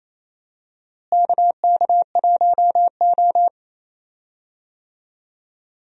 Morse code, heard as a keyed tone, about 28 wpm, 700 Hz. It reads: KK1O
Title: USA Call Signs - 4 Characters - Repeated 3x - 28wpm